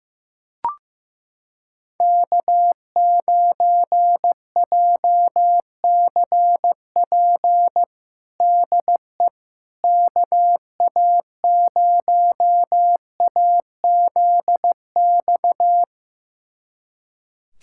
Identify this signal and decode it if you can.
Morse code, heard as a keyed tone, about 15 wpm, 700 Hz. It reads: K9JCP DE KA0AZX
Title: USA Callsign DE Callsign 15wpm